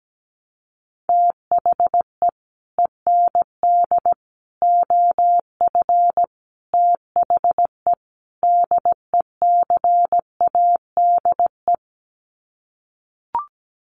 Morse code, heard as a keyed tone, about 17 wpm, 700 Hz. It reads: THE END OF THE DECADE
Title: Sets of 5 Words Encore 17wpm